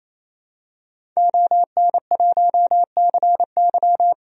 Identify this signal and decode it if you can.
Morse code, heard as a keyed tone, about 28 wpm, 700 Hz. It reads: ON1CY
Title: DXCC-100 Call Signs - 5 Characters - 28wpm